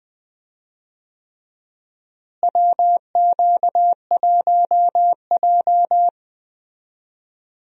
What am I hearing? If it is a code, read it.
Morse code, heard as a keyed tone, about 21 wpm, 700 Hz. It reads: WQ1J
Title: USA Call Signs - 4 Characters - Repeated 3x - 20wpm